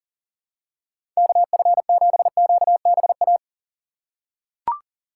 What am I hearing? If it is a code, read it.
Morse code, heard as a keyed tone, about 40 wpm, 700 Hz. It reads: KF7QBA
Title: USA Call Signs - 6x Character Spacing - 40wpm